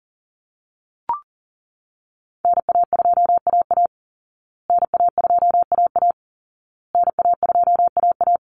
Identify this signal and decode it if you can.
Morse code, heard as a keyed tone, about 37 wpm, 700 Hz. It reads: NA2AA NA2AA NA2AA
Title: Call Signs Repeated 2x - 40wpm